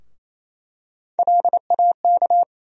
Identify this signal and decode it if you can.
Morse code, heard as a keyed tone, about 28 wpm, 700 Hz. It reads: LAK
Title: USA Call Sign Postfixes - 3 Letters 28wpm